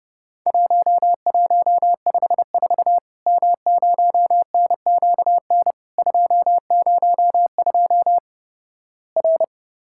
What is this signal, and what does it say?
Morse code, heard as a keyed tone, about 30 wpm, 700 Hz. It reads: 1154 M0NQN 202 R
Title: Worked All Europe DX Contest - QTC Exchange - 30wpm